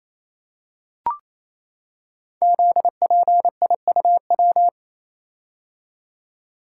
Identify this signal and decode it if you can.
Morse code, heard as a keyed tone, about 28 wpm, 700 Hz. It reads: ZPIUW
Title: Code Groups - 5 Letters - 28wpm